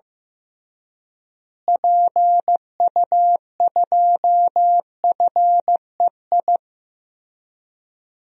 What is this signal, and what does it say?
Morse code, heard as a keyed tone, about 15 wpm, 700 Hz. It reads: PU2FEI
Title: DXCC-100 Call Signs - 6 Characters - Repeated 3x - 15wpm